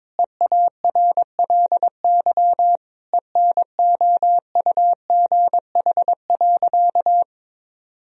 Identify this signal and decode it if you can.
Morse code, heard as a keyed tone, about 21 wpm, 700 Hz. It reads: EARLY ENOUGH.
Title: Sentences from Top 1000 Words - Review Part 2 - 22wpm